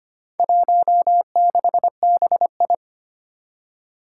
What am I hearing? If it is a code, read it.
Morse code, heard as a keyed tone, about 25 wpm, 700 Hz. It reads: 16BI
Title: Four Challenging Letter-Number Combinations 25wpm